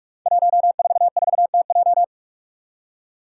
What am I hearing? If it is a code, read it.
Morse code, heard as a keyed tone, about 44 wpm, 700 Hz. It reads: 144TJ
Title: Five Challenging Letter-Number Combinations 45wpm